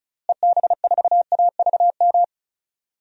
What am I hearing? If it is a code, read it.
Morse code, heard as a keyed tone, about 36 wpm, 700 Hz. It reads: EB4AVM
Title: DXCC-100 Call Signs - 6 Characters - Repeated 2x - 35wpm